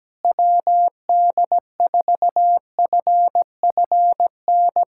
Morse code, heard as a keyed tone, about 17 wpm, 700 Hz. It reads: WD4FFN
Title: USA Callsign DE Callsign - 6 Characters - 17wpm